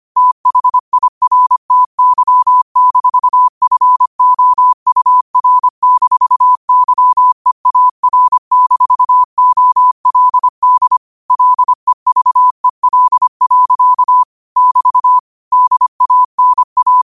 Morse code, heard as a keyed tone, about 25 wpm, 1000 Hz. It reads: THIRTY-FOUR-YEAR-OLD LEVEL. <BT> DANA
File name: qotdfiles / 25wpm